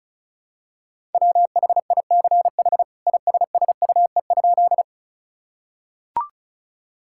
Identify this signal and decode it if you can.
Morse code, heard as a keyed tone, about 36 wpm, 700 Hz. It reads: WHICH ISSUE?
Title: Sentences from Top 1000 Words - Review Part 6 - 35wpm